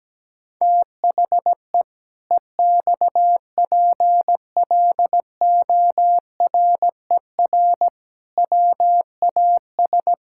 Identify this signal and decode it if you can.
Morse code, heard as a keyed tone, about 17 wpm, 700 Hz. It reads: THE EXPLORER WAS
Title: Pangram Challenge v2 - 17wpm